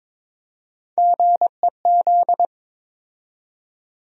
Morse code, heard as a keyed tone, about 22 wpm, 700 Hz. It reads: GEZ
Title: USA Call Sign Postfixes - 3 Letters 22wpm